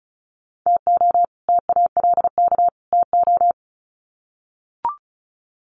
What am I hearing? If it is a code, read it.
Morse code, heard as a keyed tone, about 34 wpm, 700 Hz. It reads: TO TALK TO
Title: Sets of 3 Words 35wpm